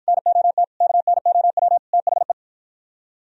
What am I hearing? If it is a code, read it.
Morse code, heard as a keyed tone, about 52 wpm, 700 Hz. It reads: NOT KNOW THE